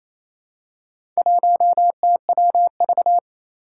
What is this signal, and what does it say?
Morse code, heard as a keyed tone, about 28 wpm, 700 Hz. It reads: 1TWV
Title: Four Challenging Letter-Number Combinations 28wpm